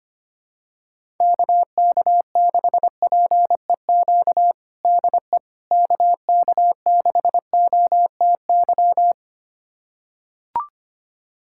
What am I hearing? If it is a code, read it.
Morse code, heard as a keyed tone, about 25 wpm, 700 Hz. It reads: KK6PEQ DE KK6OTY
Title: USA Callsign DE Callsign - 6 Characters - 25wpm